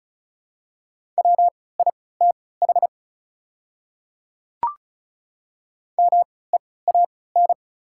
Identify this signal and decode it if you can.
Morse code, heard as a keyed tone, about 34 wpm, spaced as for 20 wpm, 690 Hz. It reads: WITH MEAN
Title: 3x Character Spacing - Top 200 Words - 35wpm